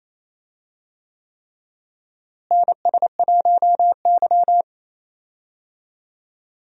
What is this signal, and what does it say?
Morse code, heard as a keyed tone, about 28 wpm, 700 Hz. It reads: NS1Y
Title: USA Call Signs - 4 Characters - Repeated 3x - 28wpm